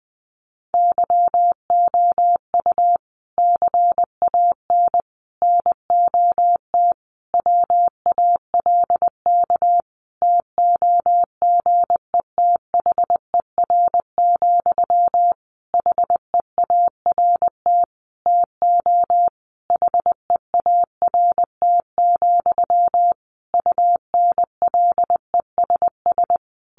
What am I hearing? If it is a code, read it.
Morse code, heard as a keyed tone, about 20 wpm, 700 Hz. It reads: YOU CAN NOT WALK TOGETHER, HEART TO HEART, UNLESS
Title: Sentences from Top 1500 Words 20wpm